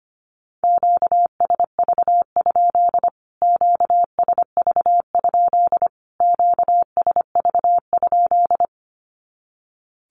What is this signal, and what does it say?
Morse code, heard as a keyed tone, about 25 wpm, 700 Hz. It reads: QSV? QSV? QSV?
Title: Q-Codes Repeated 3x 25wpm